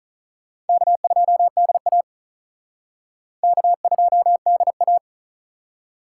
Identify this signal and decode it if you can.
Morse code, heard as a keyed tone, about 45 wpm, 700 Hz. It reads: K2DA K2DA